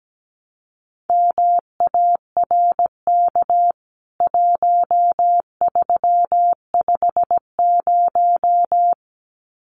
Morse code, heard as a keyed tone, about 17 wpm, 700 Hz. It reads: MARK 1350
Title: CWT Exchange 17wpm